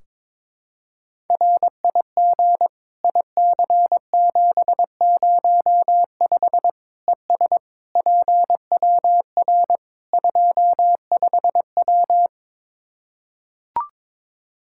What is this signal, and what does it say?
Morse code, heard as a keyed tone, about 23 wpm, 700 Hz. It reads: RIG IC705 ES PWR 25W
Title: Practice QSO - Exchange 2 - Info - 22wpm